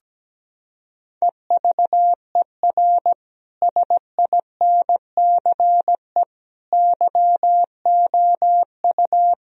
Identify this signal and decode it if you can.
Morse code, heard as a keyed tone, about 17 wpm, 700 Hz. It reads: EVER SINCE YOU
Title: Sentences from Top 1000 Words - Review Part 1 - 17wpm